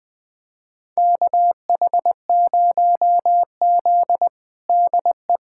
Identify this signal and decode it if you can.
Morse code, heard as a keyed tone, about 21 wpm, 700 Hz. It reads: KH0Z DE
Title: USA Callsign DE Callsign - 4 Characters - 20wpm